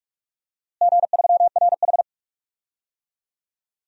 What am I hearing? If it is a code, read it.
Morse code, heard as a keyed tone, about 41 wpm, 700 Hz. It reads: G3RH
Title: DXCC-100 Call Signs - 4 Characters - Repeated 3x - 45wpm